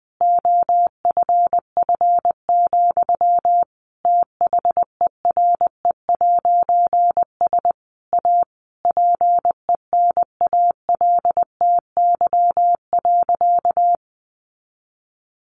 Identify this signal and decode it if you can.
Morse code, heard as a keyed tone, about 20 wpm, 700 Hz. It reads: OFF, THERE'S A PENALTY.
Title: A Sound Of Thunder 20wpm